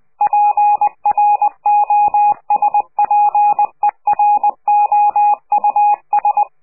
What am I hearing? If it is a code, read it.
Morse code, heard as a keyed tone, about 20 wpm, 830 Hz. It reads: PROSPEROUS
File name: CQ de PCH laatste bericht farewell